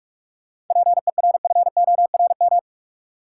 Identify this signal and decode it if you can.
Morse code, heard as a keyed tone, about 46 wpm, 700 Hz. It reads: PERFORM
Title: Sentences from Top 1000 Words - Review Part 5 - New Word Drill - 45wpm